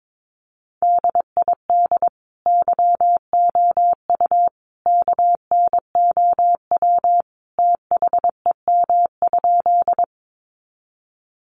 Morse code, heard as a keyed tone, about 22 wpm, 700 Hz. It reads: DID YOU KNOW THEM?
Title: Sentences from Top 100 Words 22wpm